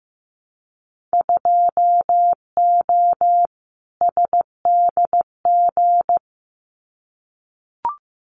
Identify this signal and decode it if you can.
Morse code, heard as a keyed tone, about 15 wpm, 700 Hz. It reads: 2O SDG
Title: Winter Field Day Exchanges 15wpm